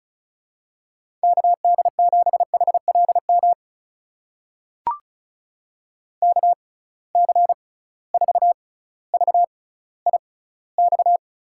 Morse code, heard as a keyed tone, about 36 wpm, spaced as for 18 wpm, 700 Hz. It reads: KD7HLM KC4VIX
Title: USA Call Signs - 6 Characters - 6x Character Spacing - 35wpm